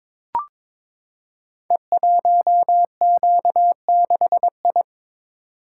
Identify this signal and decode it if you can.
Morse code, heard as a keyed tone, about 22 wpm, 700 Hz. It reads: E1Q6I
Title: Code Groups - 5 Letter:Numbers - 22wpm